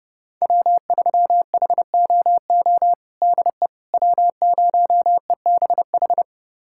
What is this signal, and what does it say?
Morse code, heard as a keyed tone, about 30 wpm, 700 Hz. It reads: W3HOO DE W0EBH
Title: USA Callsign DE Callsign - 5 Characters - 30wpm